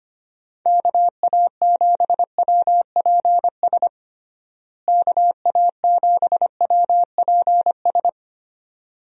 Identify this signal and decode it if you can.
Morse code, heard as a keyed tone, about 25 wpm, 700 Hz. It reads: KA7WPS KA7WPS
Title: USA Call Signs - 6 Characters - Repeated 2x - 25wpm